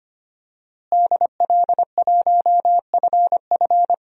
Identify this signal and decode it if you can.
Morse code, heard as a keyed tone, about 25 wpm, 700 Hz. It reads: DL1FF
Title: DXCC-100 Call Signs - 5 Characters - Repeated 2x - 25wpm